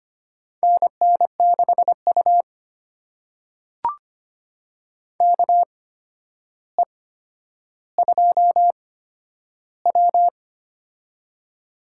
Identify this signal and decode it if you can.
Morse code, heard as a keyed tone, about 25 wpm, spaced as for 11 wpm, 700 Hz. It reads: NN6U KE2W
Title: USA Call Signs - 4 Characters - 8x Character Spacing - 25wpm